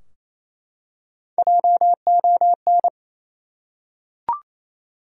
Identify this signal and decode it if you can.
Morse code, heard as a keyed tone, about 28 wpm, 700 Hz. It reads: JON
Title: USA Common Names - Max 3 Letters - 28wpm